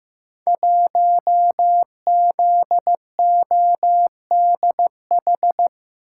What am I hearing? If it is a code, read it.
Morse code, heard as a keyed tone, about 15 wpm, 700 Hz. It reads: 1ZODH
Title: Code Groups - 5 Letter:Numbers - 15wpm